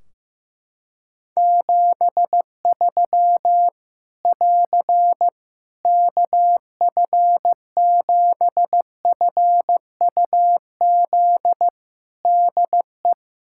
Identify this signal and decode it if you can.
Morse code, heard as a keyed tone, about 15 wpm, 700 Hz. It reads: 73 <AR> KF7FUZ DE
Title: Practice QSO - Exchange 4 - Ending - 15wpm